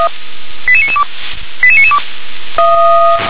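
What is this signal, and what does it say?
Five-tone selective call (ZVEI), heard as quick runs of five tones, tones of about 70 ms. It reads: 88992 88992
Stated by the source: ZVEI five-tone call received on an SDR.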